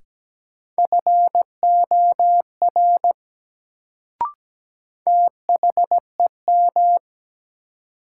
Morse code, heard as a keyed tone, about 17 wpm, 700 Hz. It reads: FOR THEM
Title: Sentences from Top 100 Words - New Word Drill - 17wpm